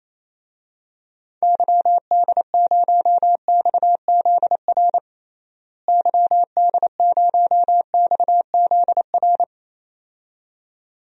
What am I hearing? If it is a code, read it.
Morse code, heard as a keyed tone, about 28 wpm, 700 Hz. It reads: YD0XZR YD0XZR
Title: DXCC-100 Call Signs - 6 Characters - Repeated 2x - 28wpm